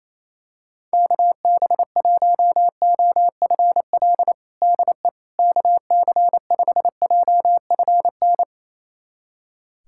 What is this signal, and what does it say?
Morse code, heard as a keyed tone, about 28 wpm, 700 Hz. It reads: KB1OFL DE KC5JFN
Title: USA Callsign DE Callsign - 6 Characters - 28wpm